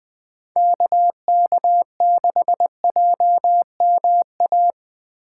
Morse code, heard as a keyed tone, about 20 wpm, 700 Hz. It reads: KK6JMA